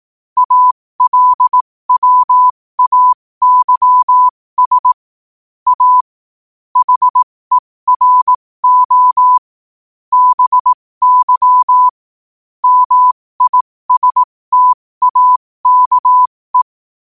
Morse code, heard as a keyed tone, about 18 wpm, spaced as for 15 wpm, 1000 Hz. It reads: ALWAYS A HERO BY MISTAKE
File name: qotdfiles / 15wpm